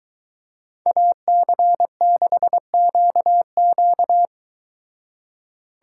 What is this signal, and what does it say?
Morse code, heard as a keyed tone, about 23 wpm, 700 Hz. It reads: AC6QQ